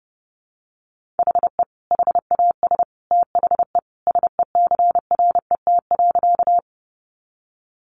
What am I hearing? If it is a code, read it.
Morse code, heard as a keyed tone, about 30 wpm, 700 Hz. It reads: HE HAS THE SECRET.
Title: Sentences from Top 1200 Words 30wpm